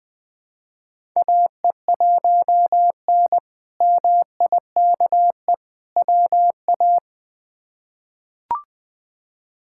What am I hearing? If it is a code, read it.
Morse code, heard as a keyed tone, about 20 wpm, 700 Hz. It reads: AE1N MIKE WA
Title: K1USN SST QSO Practice - 20wpm